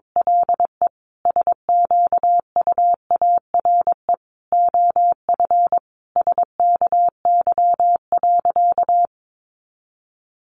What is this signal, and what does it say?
Morse code, heard as a keyed tone, about 22 wpm, 700 Hz. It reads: LE SQUARE OF SKY.
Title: Sentences from Top 1200 Words 22wpm